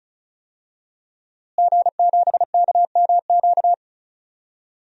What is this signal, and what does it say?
Morse code, heard as a keyed tone, about 34 wpm, 700 Hz. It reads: G7KMQ
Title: DXCC-100 Call Signs - 5 Characters - Repeated 2x - 35wpm